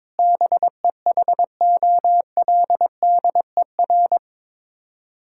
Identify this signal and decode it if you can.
Morse code, heard as a keyed tone, about 22 wpm, 700 Hz. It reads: BEHOLDER
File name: practice / 22wpm / Sets of 6 Words - 22wpm